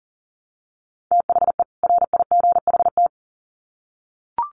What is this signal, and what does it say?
Morse code, heard as a keyed tone, about 40 wpm, 700 Hz. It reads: THE RIGHT
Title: Sets of 2 Words 40wpm